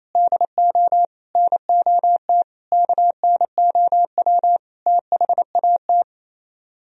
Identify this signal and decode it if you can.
Morse code, heard as a keyed tone, about 34 wpm, 700 Hz. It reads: DO NOT KNOW THAT
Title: Speed-Racing - Sets of 5 Words 28wpm